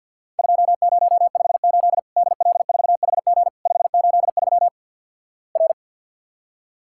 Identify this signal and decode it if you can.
Morse code, heard as a keyed tone, about 48 wpm, 700 Hz. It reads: 2058 DL4HG 583 R
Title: Worked All Europe DX Contest - QTC Exchange - 50wpm